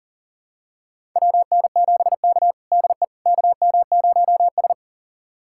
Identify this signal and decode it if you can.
Morse code, heard as a keyed tone, about 37 wpm, 700 Hz. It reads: WN7K DE KM0S
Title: USA Callsign DE Callsign - 4 Characters - 40wpm